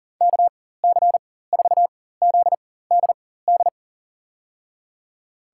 Morse code, heard as a keyed tone, about 38 wpm, spaced as for 18 wpm, 700 Hz. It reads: KC4ZDD
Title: USA Call Signs - 6 Characters - 4x Character Spacing - 40wpm